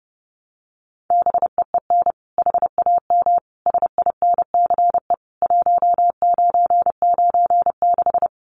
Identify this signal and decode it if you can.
Morse code, heard as a keyed tone, about 30 wpm, 700 Hz. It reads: BEEN HAM SINCE 1996
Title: Common QSO Phrases 30wpm